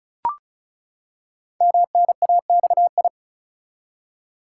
Morse code, heard as a keyed tone, about 36 wpm, 690 Hz. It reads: MNAXI
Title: Code Groups - 5 Letters - 35wpm